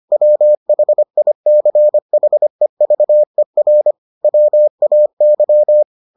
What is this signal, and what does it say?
Morse code, heard as a keyed tone, about 25 wpm, 600 Hz. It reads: WHICHEVER WAY